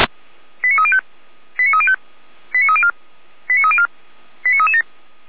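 Five-tone selective call (ZVEI), heard as quick runs of five tones, tones of about 70 ms. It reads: 89305 89305 89304 89304 89337
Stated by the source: ZVEI five-tone selective call received via SDR.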